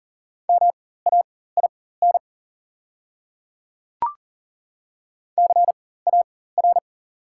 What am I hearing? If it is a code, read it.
Morse code, heard as a keyed tone, about 38 wpm, spaced as for 18 wpm, 690 Hz. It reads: MAIN CAR
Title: 4x Character Spacing - Top 300 Words - 40wpm